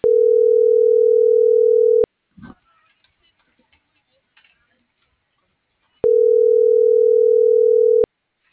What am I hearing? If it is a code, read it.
Ringback tone (North America), 440 + 480 Hz.